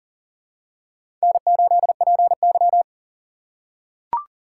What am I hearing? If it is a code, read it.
Morse code, heard as a keyed tone, about 40 wpm, 700 Hz. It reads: N8PY
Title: USA Call Signs - 2x Character Spacing - 40wpm